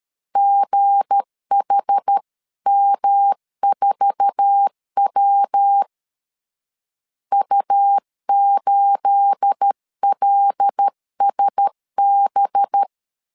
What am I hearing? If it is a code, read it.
Morse code, heard as a keyed tone, about 13 wpm, 800 Hz. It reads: GHM4W U8LSB
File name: LEZIONI / 9 28 12wpm 2351